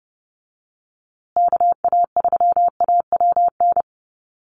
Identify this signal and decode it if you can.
Morse code, heard as a keyed tone, about 30 wpm, 700 Hz. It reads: KA3AWN
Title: USA Call Signs Repeated 2x 30wpm